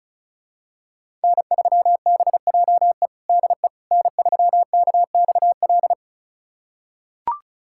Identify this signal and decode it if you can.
Morse code, heard as a keyed tone, about 36 wpm, 700 Hz. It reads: N3BJE DE N3KXL
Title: USA Callsign DE Callsign - 5 Characters - 35wpm